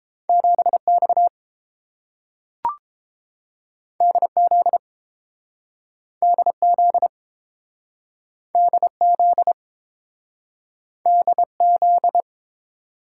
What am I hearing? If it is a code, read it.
Morse code, heard as a keyed tone, about 30 wpm, 700 Hz. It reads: 7X DZ DZ DZ DZ